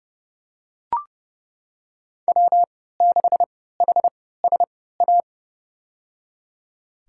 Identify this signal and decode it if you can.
Morse code, heard as a keyed tone, about 31 wpm, spaced as for 17 wpm, 700 Hz. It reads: W6HSA
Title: USA Call Signs - 5 Characters - 3x Character Spacing - 30wpm